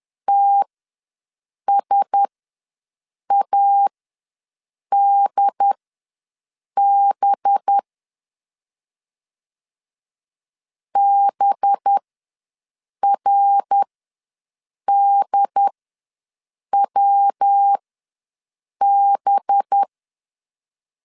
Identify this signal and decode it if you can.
Morse code, heard as a keyed tone, about 10 wpm, spaced as for 6 wpm, 800 Hz. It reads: TSADB BRDWB
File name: LEZIONI / 5 LBVC / 5 LB ws 10wpm 3616